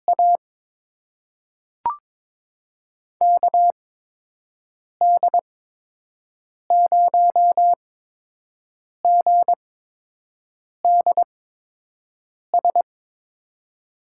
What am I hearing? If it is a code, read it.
Morse code, heard as a keyed tone, about 22 wpm, spaced as for 6 wpm, 700 Hz. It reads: A KD0GDS
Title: USA Call Signs - 6 Characters - 8x Character Spacing - 22wpm